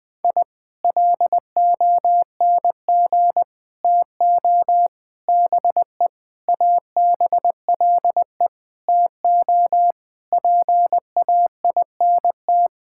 Morse code, heard as a keyed tone, about 21 wpm, 700 Hz. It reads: I LONG TO BE ABLE TO PAINT
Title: Sentences from Top 1000 Words - Review Part 6 - 20wpm